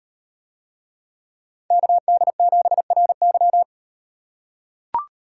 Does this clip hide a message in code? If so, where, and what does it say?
Yes: Morse code from 1.7 to 3.6 s, heard as a keyed tone, about 37 wpm, 700 Hz. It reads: KD7RY